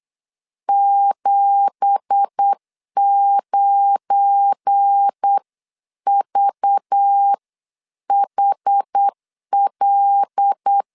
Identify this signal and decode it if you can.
Morse code, heard as a keyed tone, about 8 wpm, spaced as for 7 wpm, 800 Hz. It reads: 79VHL
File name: CW / DET / 1 9 1500C S C 8wpm 5653